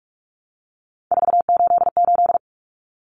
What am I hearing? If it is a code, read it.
Morse code, heard as a keyed tone, about 44 wpm, 700 Hz. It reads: 488
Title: Numbers - 3-Digit 45wpm